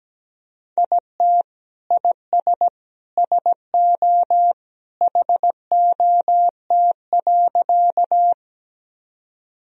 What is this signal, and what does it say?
Morse code, heard as a keyed tone, about 17 wpm, 700 Hz. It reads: IT IS SO HOT.
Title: Sentences from Top 1000 Words - Review Part 3 - 17wpm